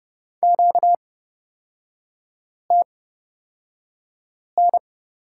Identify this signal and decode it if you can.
Morse code, heard as a keyed tone, about 31 wpm, spaced as for 5 wpm, 700 Hz. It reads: QTN
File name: Lesson 38e - Letter q - Three Characters 30wpm